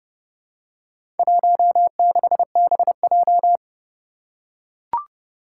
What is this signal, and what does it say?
Morse code, heard as a keyed tone, about 30 wpm, 700 Hz. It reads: 16BJ
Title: Four Challenging Letter-Number Combinations 30wpm